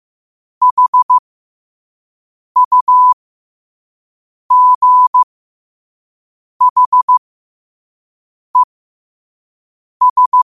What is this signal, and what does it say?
Morse code, heard as a keyed tone, about 15 wpm, spaced as for 5 wpm, 1000 Hz. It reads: HUGHES
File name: qotdfiles / 5wpm